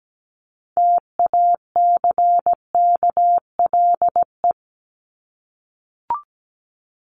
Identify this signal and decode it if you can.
Morse code, heard as a keyed tone, about 17 wpm, 700 Hz. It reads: TACKLE
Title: Top 1751-2000 Words 17wpm